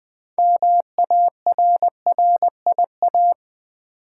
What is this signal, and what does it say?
Morse code, heard as a keyed tone, about 20 wpm, 700 Hz. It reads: MARRIA
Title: Build A Word - 2 Letters At A Time - 8-Letter Words - 20wpm